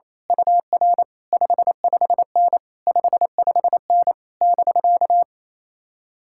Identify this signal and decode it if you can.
Morse code, heard as a keyed tone, about 27 wpm, 700 Hz. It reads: UR 55N 55N <BK>
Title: Speed-Racing Common QSO Phrases Encore - 28wpm